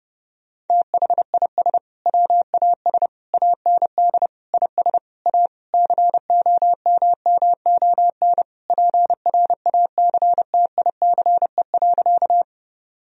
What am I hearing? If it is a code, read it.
Morse code, heard as a keyed tone, about 32 wpm, 700 Hz. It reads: THIS WAS AND IS A COMMON PRACTICE.
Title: Sentences from Top 1000 Words - Review Part 4 - 30wpm